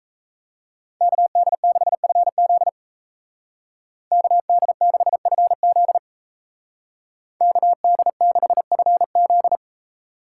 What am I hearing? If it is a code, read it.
Morse code, heard as a keyed tone, about 34 wpm, 700 Hz. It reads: KD6FZ KD6FZ KD6FZ
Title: Speed Racing - USA 2x2 Call Signs 28wpm